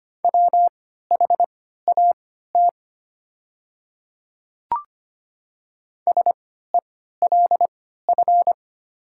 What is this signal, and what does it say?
Morse code, heard as a keyed tone, about 25 wpm, spaced as for 14 wpm, 700 Hz. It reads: WHAT SELF
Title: 3x Character Spacing - Top 300 Words - 25wpm